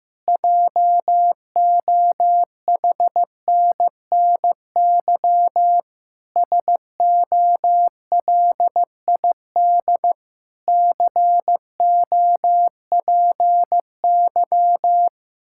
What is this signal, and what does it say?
Morse code, heard as a keyed tone, about 15 wpm, 700 Hz. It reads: JOHNNY SOLID COPY
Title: Practice QSO - Exchange 3 - More Info - 15wpm